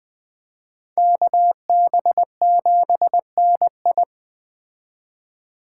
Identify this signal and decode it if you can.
Morse code, heard as a keyed tone, about 20 wpm, 700 Hz. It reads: KB7NI